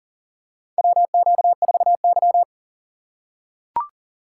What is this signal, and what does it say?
Morse code, heard as a keyed tone, about 37 wpm, 700 Hz. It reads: WQ4Y
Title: USA Call Signs - 4 Characters - 6x Character Spacing - 40wpm